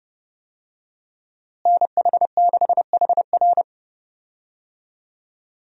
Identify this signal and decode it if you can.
Morse code, heard as a keyed tone, about 30 wpm, 700 Hz. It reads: NH6HR
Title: USA Call Signs - 5 Characters - Repeated 3x - 30wpm